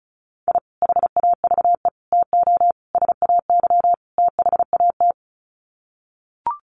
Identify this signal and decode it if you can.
Morse code, heard as a keyed tone, about 34 wpm, 700 Hz. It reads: I HAVE TO SAY THAT
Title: Sets of 5 Words Encore 35wpm